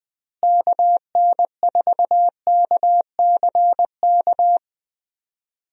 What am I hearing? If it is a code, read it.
Morse code, heard as a keyed tone, about 20 wpm, 700 Hz. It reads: KN4KCK
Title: USA Call Signs - 6 Characters - Repeated 2x - 20wpm